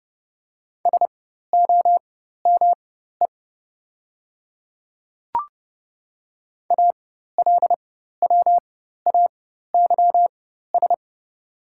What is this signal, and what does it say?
Morse code, heard as a keyed tone, about 30 wpm, spaced as for 14 wpm, 700 Hz. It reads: SOME ALWAYS
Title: 4x Character Spacing - Top 300 Words - 30wpm